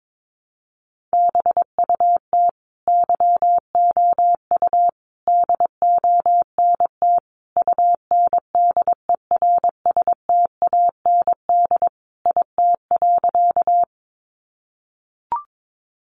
Morse code, heard as a keyed tone, about 22 wpm, 700 Hz. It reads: BUT YOU DONT UNDERSTAND IT.
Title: Sentences from Top 600 Words 22wpm